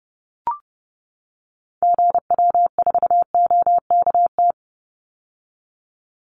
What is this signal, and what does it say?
Morse code, heard as a keyed tone, about 30 wpm, 700 Hz. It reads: GW4OKT
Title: Call Signs - 30wpm